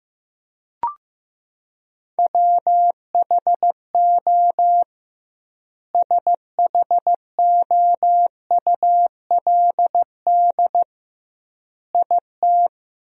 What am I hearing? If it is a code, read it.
Morse code, heard as a keyed tone, about 15 wpm, 700 Hz. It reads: WHO SHOULD IT
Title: 2x Word-Spacing - Sentences from Top 300 Words - 15wpm